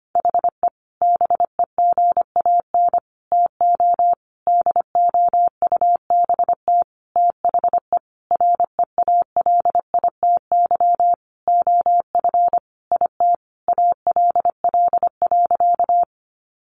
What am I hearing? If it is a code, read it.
Morse code, heard as a keyed tone, about 25 wpm, 700 Hz. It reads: HE BEGAN TO DOUBT THE REALITY OF IT ALL.
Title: Sentences from Top 1500 Words 25wpm